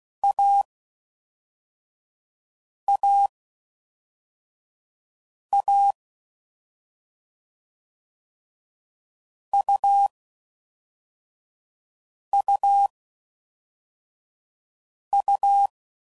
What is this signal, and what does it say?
Morse code, heard as a keyed tone, about 16 wpm, spaced as for 4 wpm, 800 Hz. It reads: AAA UUU
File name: Combinations / Difficult Combos / AUV AUV4